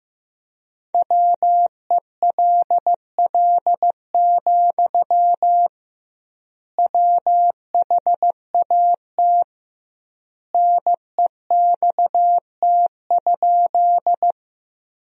Morse code, heard as a keyed tone, about 15 wpm, 700 Hz. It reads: WELL, WHAT NEXT?
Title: 2x Word-Spacing - Sentences from Top 300 Words - 15wpm